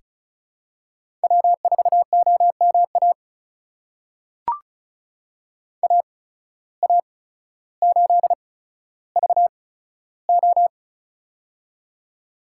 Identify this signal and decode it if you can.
Morse code, heard as a keyed tone, about 36 wpm, spaced as for 15 wpm, 700 Hz. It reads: W4OMA AA8VO
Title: USA Call Signs - 5 Characters - 8x Character Spacing - 35wpm